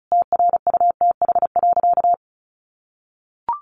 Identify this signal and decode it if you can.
Morse code, heard as a keyed tone, about 34 wpm, 700 Hz. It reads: TRUTH.
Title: Sentences from Top 1400 Words 35wpm